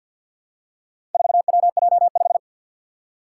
Morse code, heard as a keyed tone, about 53 wpm, 700 Hz. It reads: 4WJ5
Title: Four Challenging Letter-Number Combinations 50wpm